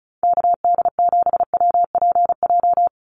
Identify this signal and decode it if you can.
Morse code, heard as a keyed tone, about 34 wpm, 700 Hz. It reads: KD7WPJ
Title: NA SOTA Activator Callsigns 35wpm